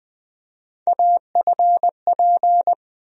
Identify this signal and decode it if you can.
Morse code, heard as a keyed tone, about 20 wpm, 700 Hz. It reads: AFP
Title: USA Call Sign Postfixes - 3 Letters 20wpm